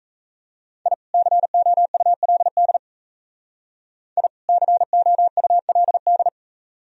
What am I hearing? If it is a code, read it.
Morse code, heard as a keyed tone, about 33 wpm, 700 Hz. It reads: I COULD I COULD
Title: Speed-Racing - Sets of 2 Words 28wpm